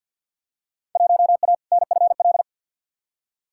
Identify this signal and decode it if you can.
Morse code, heard as a keyed tone, about 52 wpm, 700 Hz. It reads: 1A NFL
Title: Field Day Exchanges - 50wpm